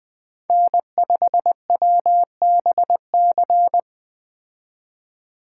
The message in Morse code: N5WBC

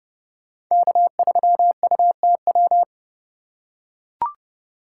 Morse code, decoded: K3UTW

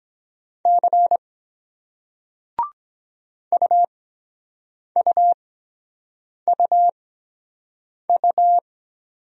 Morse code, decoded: C UUUU

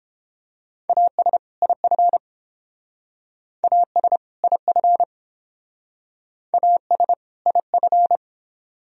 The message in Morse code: AS IF AS IF AS IF

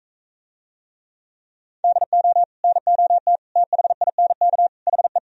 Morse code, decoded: DO NOT THINK HE

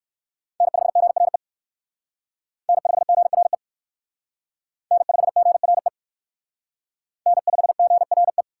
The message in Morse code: N5GRE N5GRE N5GRE N5GRE

